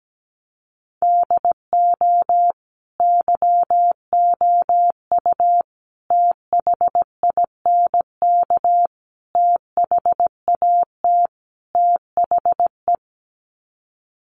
Morse code, heard as a keyed tone, about 17 wpm, 700 Hz. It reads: DO YOU THINK THAT THE
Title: Sets of 5 Words Encore 17wpm